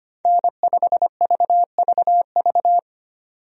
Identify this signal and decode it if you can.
Morse code, heard as a keyed tone, about 25 wpm, 700 Hz. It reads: N5VVV
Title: USA Call Signs - 5 Characters - 25wpm